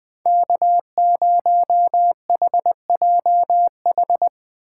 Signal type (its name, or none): Morse code